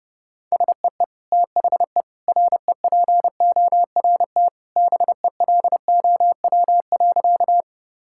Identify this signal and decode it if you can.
Morse code, heard as a keyed tone, about 32 wpm, 700 Hz. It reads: SEE THE REPORT BELOW.
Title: Sentences from Top 1000 Words - Review Part 6 - 30wpm